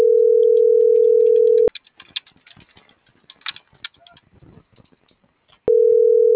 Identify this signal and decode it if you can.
Ringback tone (North America), 440 + 480 Hz.